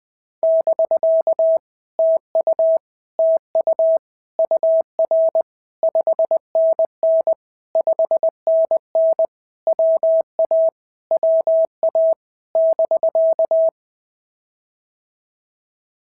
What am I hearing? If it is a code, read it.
Morse code, heard as a keyed tone, about 18 wpm, 650 Hz. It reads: <BK> TU TU UR 5NN 5NN WA WA <BK>
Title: POTA QSO Practice - 20wpm